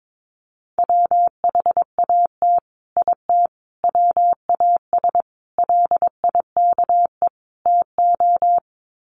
Morse code, heard as a keyed tone, about 22 wpm, 700 Hz. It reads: WHAT IT WAS LIKE TO
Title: Sets of 5 Words Encore 22wpm